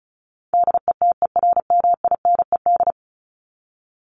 Morse code, heard as a keyed tone, about 34 wpm, 700 Hz. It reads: DETERMINED
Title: ED Suffix - Top 100 Words with Base Words 35wpm